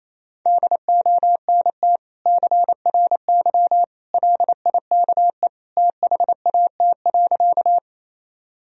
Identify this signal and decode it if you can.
Morse code, heard as a keyed tone, about 27 wpm, 700 Hz. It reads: DONT CRY LIKE THAT.
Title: Sentences from Top 1000 Words - Review Part 2 - 28wpm